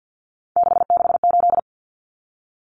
Morse code, heard as a keyed tone, about 50 wpm, 700 Hz. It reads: 668
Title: Common Cut Numbers - 3-Digit 50wpm